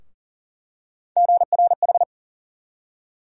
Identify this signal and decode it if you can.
Morse code, heard as a keyed tone, about 40 wpm, 700 Hz. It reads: GRH
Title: USA Call Sign Postfixes - 3 Letters 40wpm